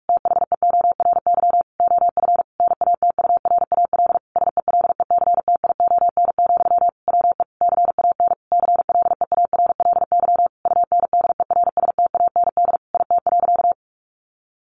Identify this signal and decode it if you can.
Morse code, heard as a keyed tone, about 44 wpm, 700 Hz. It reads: THEORY OF NATURAL SELECTION, WE CAN CLEARLY UNDERSTAND IT.
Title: Sentences from Top 1400 Words 45wpm